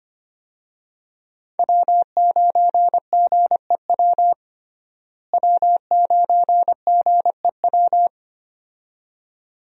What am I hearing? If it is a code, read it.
Morse code, heard as a keyed tone, about 25 wpm, 700 Hz. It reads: W9GEW W9GEW
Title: USA Call Signs - 5 Characters - Repeated 2x - 25wpm